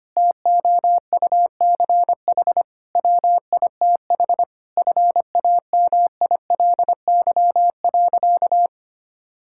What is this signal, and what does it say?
Morse code, heard as a keyed tone, about 25 wpm, 700 Hz. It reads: TOUCH WITH FAMILY.
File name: practice / 25wpm / Difficult Sentences with Hints - 25wpm